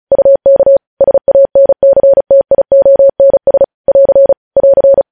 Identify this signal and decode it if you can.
Morse code, heard as a keyed tone, about 36 wpm, 550 Hz. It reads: UK SANCTIONS <AR> <AR>
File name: podcast / News headlines-1x-35x35